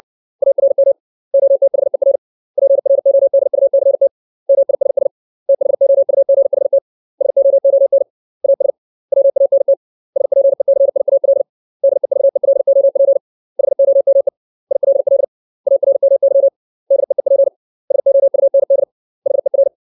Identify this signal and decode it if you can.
Morse code, heard as a keyed tone, about 60 wpm, 550 Hz. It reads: ARM OTHER PRODUCT MISS THOUGHT SOON AS WANT SPECIAL DEVELOP HOME ILL ARMY DEEP SOUND HER
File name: 632 words / 632 Words 60WPM